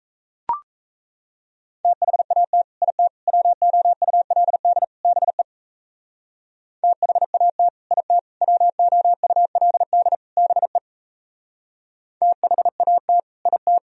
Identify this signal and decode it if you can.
Morse code, heard as a keyed tone, about 36 wpm, 700 Hz. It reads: THAT IT WOULD BE THAT IT WOULD BE THAT IT